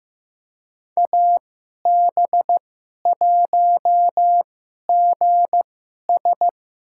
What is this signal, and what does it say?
Morse code, heard as a keyed tone, about 15 wpm, spaced as for 11 wpm, 700 Hz. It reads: AB1GS
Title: USA Call Signs - 5 Characters - 2x Character Spacing - 15wpm